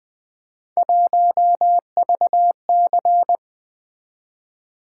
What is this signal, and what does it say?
Morse code, heard as a keyed tone, about 20 wpm, 700 Hz. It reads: 1VC
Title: Three Letter-Number Combinations - 20wpm